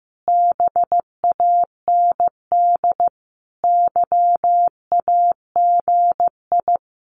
Morse code, heard as a keyed tone, about 14 wpm, 700 Hz. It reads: BAND YAGI
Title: Common QSO Phrases 15wpm